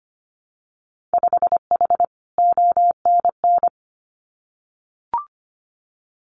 Morse code, heard as a keyed tone, about 25 wpm, 700 Hz. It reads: 5H ONN
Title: Winter Field Day Exchanges Encore 25wpm